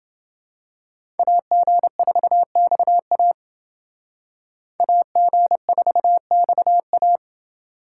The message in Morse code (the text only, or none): AG4XA AG4XA